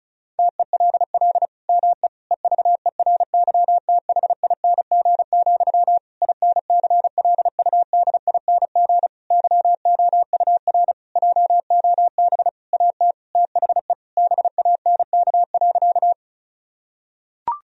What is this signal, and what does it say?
Morse code, heard as a keyed tone, about 34 wpm, 700 Hz. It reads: TELL ME EVERYTHING, INCLUDING YOUR JOB AT THE BANK.
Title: Sentences from Top 1000 Words - Review Part 4 - 35wpm